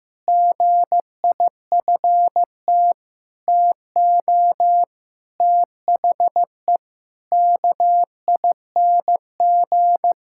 Morse code, heard as a keyed tone, about 15 wpm, 700 Hz. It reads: GIFT TO THE KING
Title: Sentences from Top 1000 Words - Review Part 6 - 15wpm